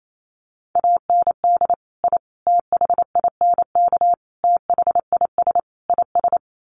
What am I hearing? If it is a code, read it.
Morse code, heard as a keyed tone, about 28 wpm, 700 Hz. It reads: AND I THINK THIS IS
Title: Sets of 5 Words Encore 28wpm